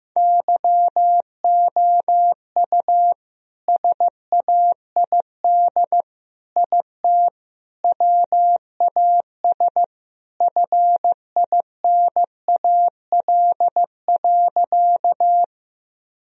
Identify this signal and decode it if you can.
Morse code, heard as a keyed tone, about 15 wpm, 700 Hz. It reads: YOU SAID IT WAS FINAL.
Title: Sentences from Top 1000 Words - Review Part 2 - 15wpm